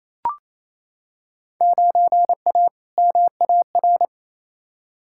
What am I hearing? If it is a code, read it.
Morse code, heard as a keyed tone, about 28 wpm, 700 Hz. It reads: 9A MAR
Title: Field Day Exchanges Encore - 28wpm